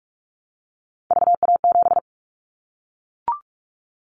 Morse code, heard as a keyed tone, about 44 wpm, 700 Hz. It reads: VA7
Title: Callsign Prefixes for Top 25 DXCC - Repeated 3x - 45wpm